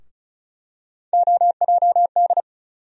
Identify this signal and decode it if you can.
Morse code, heard as a keyed tone, about 34 wpm, 700 Hz. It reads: OJD
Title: USA Call Sign Postfixes - 3 Letters 35wpm